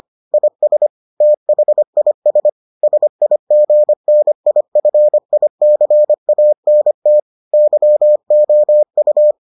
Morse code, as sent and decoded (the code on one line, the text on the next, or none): .. ... / - .... .. ... / ... .. --. -. .. ..-. .. -.-. .- -. - / -.-- --- ..-
IS THIS SIGNIFICANT YOU